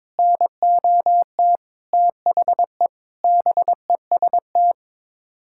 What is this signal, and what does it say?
Morse code, heard as a keyed tone, about 22 wpm, 700 Hz. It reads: NOT THE BEST
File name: practice / 22wpm / Sets of 8 Words - 22wpm